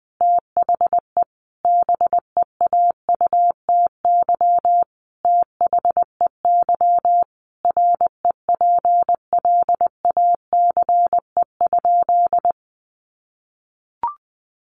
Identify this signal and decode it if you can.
Morse code, heard as a keyed tone, about 20 wpm, 700 Hz. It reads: THE BEAUTY THEY REPLACE?
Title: Sentences from Top 1400 Words 20wpm